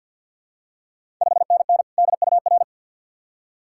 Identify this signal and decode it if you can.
Morse code, heard as a keyed tone, about 46 wpm, 690 Hz. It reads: 5NN DUR
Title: Ontario QSO Party Exchange 50wpm